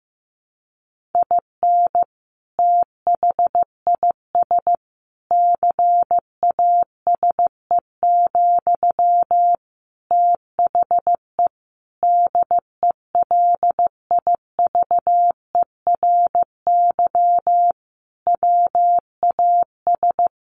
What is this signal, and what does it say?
Morse code, heard as a keyed tone, about 15 wpm, 700 Hz. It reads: IN THIS CASE, THE DELIVERY WAS
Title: Sentences from Top 1500 Words 15wpm